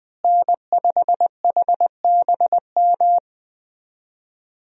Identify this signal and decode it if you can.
Morse code, heard as a keyed tone, about 21 wpm, 700 Hz. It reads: N5HBM
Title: USA Call Signs - 5 Characters - Repeated 3x - 20wpm